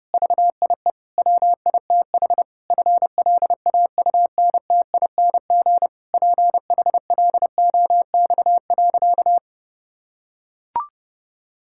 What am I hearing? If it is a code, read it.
Morse code, heard as a keyed tone, about 30 wpm, 700 Hz. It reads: VIE WITH FLAUNTING PHLOX.
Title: Pangram Challenge v2 - 30wpm